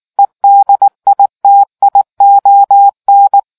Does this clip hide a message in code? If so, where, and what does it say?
Yes: Morse code from 0.2 to 3.4 s, heard as a keyed tone, about 19 wpm, 800 Hz. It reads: EDITION